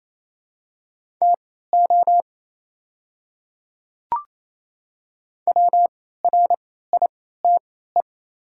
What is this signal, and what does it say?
Morse code, heard as a keyed tone, about 27 wpm, spaced as for 16 wpm, 700 Hz. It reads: TO WRITE
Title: 3x Character Spacing - Top 100 Words - 28wpm